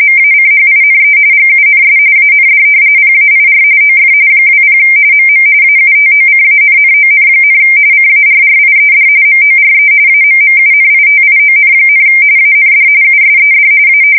RTTY, about 45.45 baud. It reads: MACHINE GUNS AND PAIR OF 20 MM AUTOCANNON, CARRIED MORE AMMUNITION, AND USED A
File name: fsk170 / usnavy4